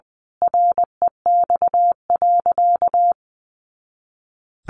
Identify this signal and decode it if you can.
Morse code, heard as a keyed tone, about 20 wpm, 700 Hz. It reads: REX.
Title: A Sound Of Thunder 20wpm